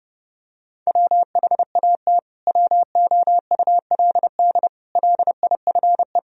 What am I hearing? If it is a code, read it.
Morse code, heard as a keyed tone, about 30 wpm, 700 Hz. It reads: WHAT WOULD LIFE
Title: Sentences from Top 1000 Words - Review Part 3 - 30wpm